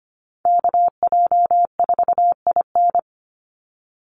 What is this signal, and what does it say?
Morse code, heard as a keyed tone, about 25 wpm, 700 Hz. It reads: KJ4IN
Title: USA Call Signs 25wpm